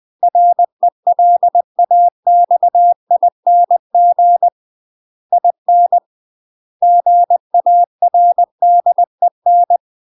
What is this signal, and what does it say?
Morse code, heard as a keyed tone, about 20 wpm, 700 Hz. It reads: RELAXING IN GARDEN
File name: UK MASTER QSOs TRAINING/20WPM